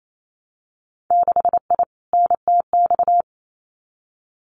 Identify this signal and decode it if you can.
Morse code, heard as a keyed tone, about 28 wpm, 700 Hz. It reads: 6I NTX
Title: Winter Field Day Exchanges Encore 28wpm